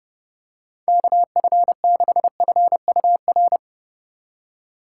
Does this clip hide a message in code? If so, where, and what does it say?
Yes: Morse code from 0.9 to 3.6 s, heard as a keyed tone, about 32 wpm, 700 Hz. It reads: KF6FUR